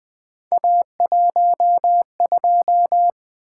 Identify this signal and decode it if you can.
Morse code, heard as a keyed tone, about 20 wpm, 700 Hz. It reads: A12
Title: Three Letter-Number Combinations - 20wpm